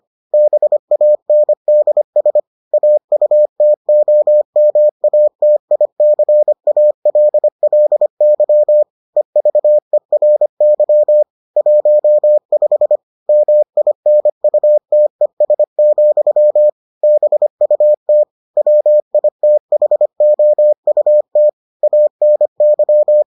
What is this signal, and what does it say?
Morse code, heard as a keyed tone, about 25 wpm, 600 Hz. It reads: BANDS AUTOMATICALLY EVERY 15 MINUTES, BUT WITHOUT ANY